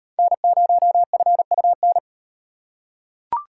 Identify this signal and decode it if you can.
Morse code, heard as a keyed tone, about 38 wpm, 700 Hz. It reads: N0FUN